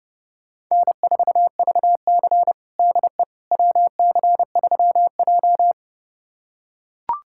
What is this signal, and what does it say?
Morse code, heard as a keyed tone, about 30 wpm, 700 Hz. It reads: N4VC DE WC3J
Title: USA Callsign DE Callsign - 4 Characters - 30wpm